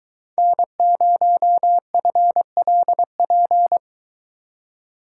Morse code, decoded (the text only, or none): N0FLP